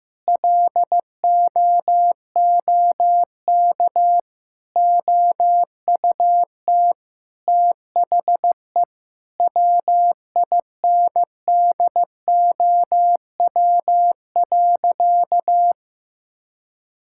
Morse code, heard as a keyed tone, about 15 wpm, 700 Hz. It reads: LOOK OUT THE WINDOW.
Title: Sentences from Top 1600 Words 15wpm